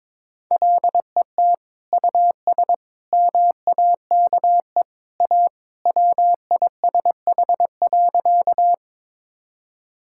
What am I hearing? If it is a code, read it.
Morse code, heard as a keyed tone, about 22 wpm, 700 Hz. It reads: LET US MAKE A WISH.
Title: Sentences from Top 1600 Words 22wpm